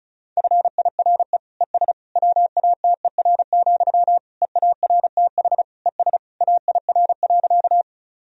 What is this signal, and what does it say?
Morse code, heard as a keyed tone, about 34 wpm, 700 Hz. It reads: FIRE ES WATER, EARTH ES AIR.
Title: Sentences from Top 400 Words and Abbreviations 35wpm